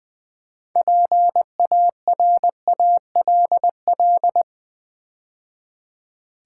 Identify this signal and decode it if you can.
Morse code, heard as a keyed tone, about 20 wpm, 700 Hz. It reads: PARALL
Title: Build A Word - 3 Letters At A Time - 8-Letter Words - 20wpm